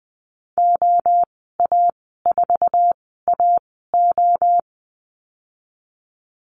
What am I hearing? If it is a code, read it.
Morse code, heard as a keyed tone, about 20 wpm, spaced as for 15 wpm, 700 Hz. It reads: OA4AO
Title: 062 Lesson 11g - Letter h - Call Signs 20wpm